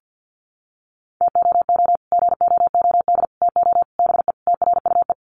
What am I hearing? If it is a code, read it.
Morse code, heard as a keyed tone, about 49 wpm, 700 Hz. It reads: TOO GOOD TO BE TRUE
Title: Sets of 5 Words Encore 50wpm